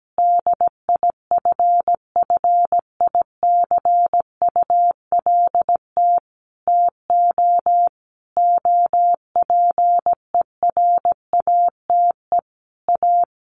Morse code, heard as a keyed tone, about 17 wpm, 700 Hz. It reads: DIFFICULT TO OPERATE A
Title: Sentences from Top 1500 Words 17wpm